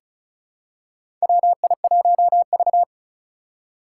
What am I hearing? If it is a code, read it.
Morse code, heard as a keyed tone, about 34 wpm, 700 Hz. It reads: WI1V
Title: USA Call Signs - 4 Characters - 2x Character Spacing - 35wpm